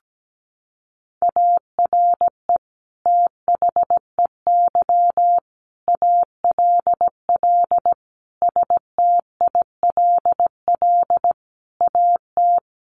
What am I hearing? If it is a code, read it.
Morse code, heard as a keyed tone, about 17 wpm, 700 Hz. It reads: ARE THEY ALL STILL AT
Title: Sentences from Top 1500 Words 17wpm